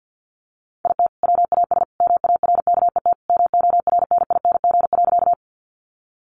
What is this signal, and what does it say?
Morse code, heard as a keyed tone, about 50 wpm, 700 Hz. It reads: IT WAS MARKET MORNING.
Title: Sentences from Top 600 Words 50wpm